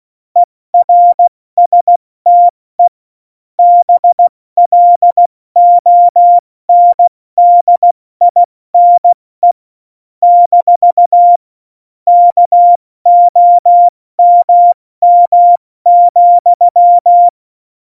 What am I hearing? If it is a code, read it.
Morse code, heard as a keyed tone, about 16 wpm, 700 Hz. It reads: ERSTE BLONDINE - KOMM,
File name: Morsekurs / Witze / Witz 18 Juli 17, Tempo 70 F80